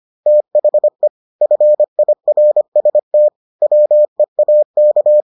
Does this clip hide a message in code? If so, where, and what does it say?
Yes: Morse code from 0.3 to 5.2 s, heard as a keyed tone, about 25 wpm, 600 Hz. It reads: THE FIRST WEAK